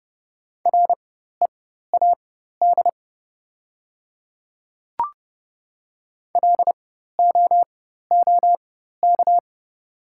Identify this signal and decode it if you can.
Morse code, heard as a keyed tone, about 30 wpm, spaced as for 14 wpm, 700 Hz. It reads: READ LOOK
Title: 4x Character Spacing - Top 200 Words - 30wpm